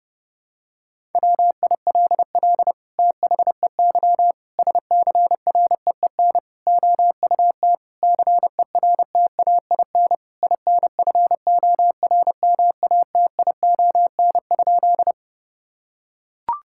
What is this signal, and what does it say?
Morse code, heard as a keyed tone, about 30 wpm, 700 Hz. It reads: WILL THEY SCREEN OUT CERTAIN INFORMATION?
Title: Sentences from Top 1000 Words - Review Part 6 - 30wpm